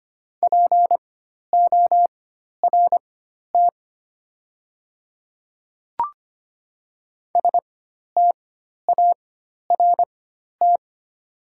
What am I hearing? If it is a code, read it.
Morse code, heard as a keyed tone, about 25 wpm, spaced as for 12 wpm, 690 Hz. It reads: PORT START
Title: 4x Character Spacing - Top 300 Words - 25wpm